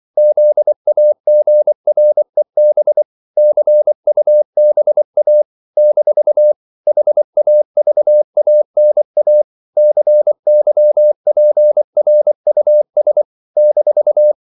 Morse code, heard as a keyed tone, about 24 wpm, 600 Hz. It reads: ZAGREB CUBA - HAVANA CYPRUS -
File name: CW / Practice / NationalCapitals24